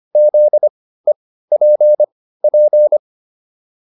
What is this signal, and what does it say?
Morse code, heard as a keyed tone, about 25 wpm, spaced as for 15 wpm, 600 Hz. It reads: ZEPP
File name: QSO401 15